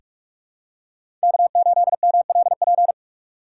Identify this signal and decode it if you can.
Morse code, heard as a keyed tone, about 46 wpm, 700 Hz. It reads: K8MLP
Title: USA Call Signs - 5 Characters - 2x Character Spacing - 45wpm